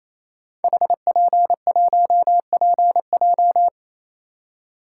Morse code, heard as a keyed tone, about 29 wpm, 700 Hz. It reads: HP1PJ
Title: Code Groups - 5 Letter:Numbers - 28wpm